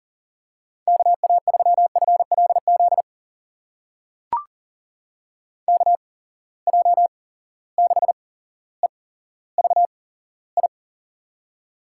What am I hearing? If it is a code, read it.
Morse code, heard as a keyed tone, about 42 wpm, spaced as for 17 wpm, 700 Hz. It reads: KA3FLZ KJ6EVI
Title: USA Call Signs - 6 Characters - 8x Character Spacing - 40wpm